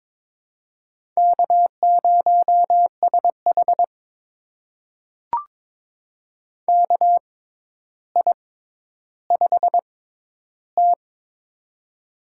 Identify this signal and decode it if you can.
Morse code, heard as a keyed tone, about 22 wpm, spaced as for 11 wpm, 700 Hz. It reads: K0SH KI5T
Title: USA Call Signs - 4 Characters - 6x Character Spacing - 22wpm